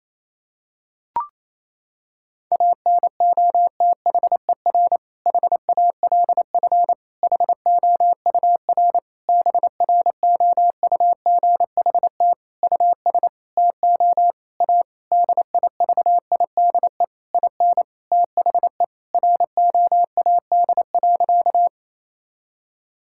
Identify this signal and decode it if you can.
Morse code, heard as a keyed tone, about 28 wpm, 700 Hz. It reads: ANOTHER HALF HOUR BROUGHT US TO A DIVIDE IN THE ROAD.
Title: Sentences from Top 1600 Words 28wpm